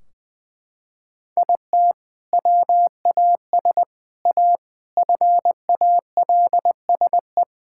Morse code, heard as a keyed tone, about 20 wpm, 700 Hz. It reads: IT WAS A FALSE
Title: Sentences from Top 1600 Words 20wpm